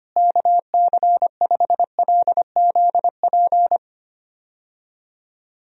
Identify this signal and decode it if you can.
Morse code, heard as a keyed tone, about 25 wpm, 700 Hz. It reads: KC5LZP
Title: USA Call Signs - 6 Characters - Repeated 2x - 25wpm